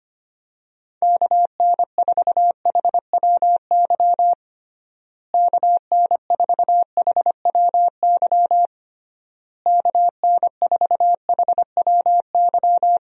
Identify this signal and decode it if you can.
Morse code, heard as a keyed tone, about 25 wpm, 700 Hz. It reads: KN4HWY KN4HWY KN4HWY
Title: USA Call Signs - 6 Characters - Repeated 3x - 25wpm